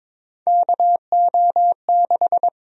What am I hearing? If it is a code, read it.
Morse code, heard as a keyed tone, about 21 wpm, 700 Hz. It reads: KO6
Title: USA Call Sign Prefixes - 22wpm